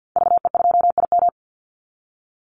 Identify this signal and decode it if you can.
Morse code, heard as a keyed tone, about 50 wpm, 700 Hz. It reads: VE2IM
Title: Call Signs Repeated 2x - 50wpm